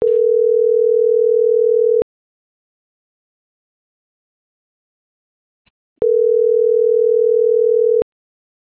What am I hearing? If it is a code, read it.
Ringback tone (North America), 440 + 480 Hz.